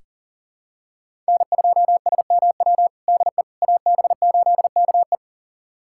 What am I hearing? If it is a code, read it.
Morse code, heard as a keyed tone, about 40 wpm, 700 Hz. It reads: N2SMW DE AB8KE
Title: USA Callsign DE Callsign - 5 Characters - 40wpm